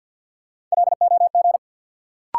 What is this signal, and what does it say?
Morse code, heard as a keyed tone, about 51 wpm, 700 Hz. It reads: LOG
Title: Sentences from Top 1400 Words - New Word Drill Alt - 50wpm